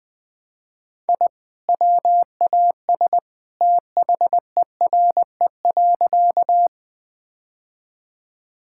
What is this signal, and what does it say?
Morse code, heard as a keyed tone, about 19 wpm, 700 Hz. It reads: I WAS THERE.
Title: Sentences with 2-Word Phrases - 20wpm